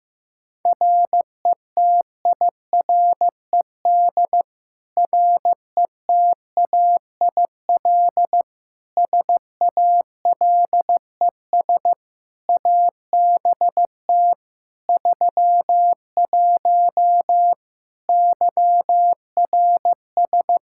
Morse code, heard as a keyed tone, about 15 wpm, 700 Hz. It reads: RETIRED RETAIL SALES ABT 31 YRS
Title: Practice QSO - Exchange 3 - More Info - 15wpm